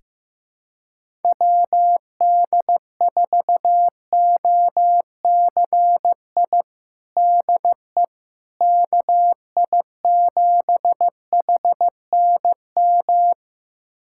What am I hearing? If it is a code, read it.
Morse code, heard as a keyed tone, about 15 wpm, 700 Hz. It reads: WD4OCI DE KI7HNM
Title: USA Callsign DE Callsign - 6 Characters - 15wpm